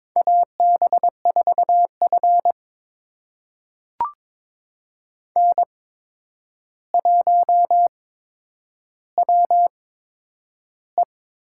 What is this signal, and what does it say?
Morse code, heard as a keyed tone, about 22 wpm, spaced as for 9 wpm, 700 Hz. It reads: AB4F N1WE
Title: USA Call Signs - 8x Character Spacing - 22wpm